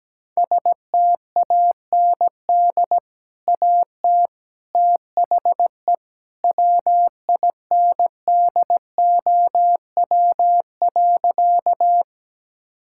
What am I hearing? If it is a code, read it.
Morse code, heard as a keyed tone, about 17 wpm, 700 Hz. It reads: STAND AT THE WINDOW.
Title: Sentences from Top 1600 Words 17wpm